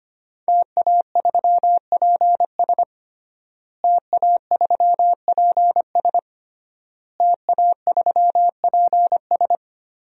Morse code, heard as a keyed tone, about 25 wpm, 700 Hz. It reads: TA3PS TA3PS TA3PS
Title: DXCC-100 Call Signs - 5 Characters - Repeated 3x - 25wpm